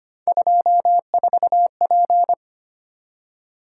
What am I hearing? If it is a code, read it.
Morse code, heard as a keyed tone, about 25 wpm, 700 Hz. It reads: 24P
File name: Three Letter-Number Combinations - 25wpm